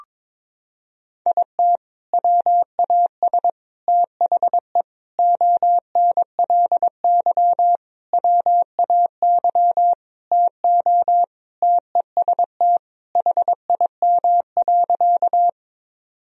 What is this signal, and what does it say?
Morse code, heard as a keyed tone, about 22 wpm, 700 Hz. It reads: IT WAS THE ONLY WAY TO TEST HIM.
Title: Sentences from Top 1000 Words - Review Part 3 - 22wpm